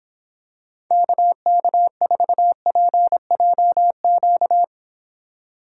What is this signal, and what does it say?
Morse code, heard as a keyed tone, about 26 wpm, 700 Hz. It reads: KK4PJQ